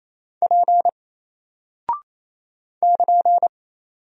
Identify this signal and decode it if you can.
Morse code, heard as a keyed tone, about 28 wpm, 700 Hz. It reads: P <KN>
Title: Single Character (Letter:Number:Punctuation:Prosign) Alt 28wpm